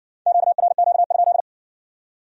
Morse code, heard as a keyed tone, about 57 wpm, 700 Hz. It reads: QRQ?